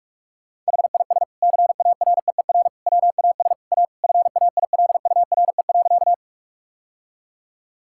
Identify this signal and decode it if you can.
Morse code, heard as a keyed tone, about 46 wpm, 700 Hz. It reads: HIS CAREER WAS A FAILURE.
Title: Sentences from Top 1000 Words - Review Part 6 - 45wpm